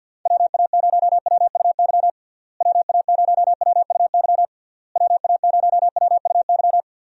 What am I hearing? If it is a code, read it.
Morse code, heard as a keyed tone, about 50 wpm, 700 Hz. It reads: WA0WUY WA0WUY WA0WUY
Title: USA Call Signs - 6 Characters - Repeated 3x - 50wpm